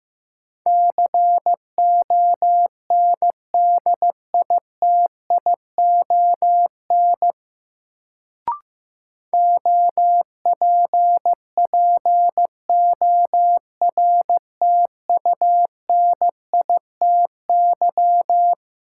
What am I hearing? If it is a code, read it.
Morse code, heard as a keyed tone, about 15 wpm, 700 Hz. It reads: CONDITION OPPORTUNITY
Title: Sentences from Top 1000 Words - New Word Drill - 15wpm